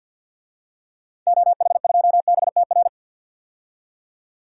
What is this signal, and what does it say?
Morse code, heard as a keyed tone, about 47 wpm, 700 Hz. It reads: OH2BTR
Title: DXCC-100 Call Signs - 6 Characters - 50wpm